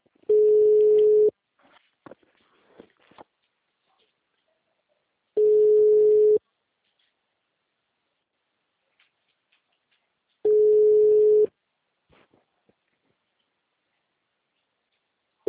Ringback tone (Europe, 425 Hz).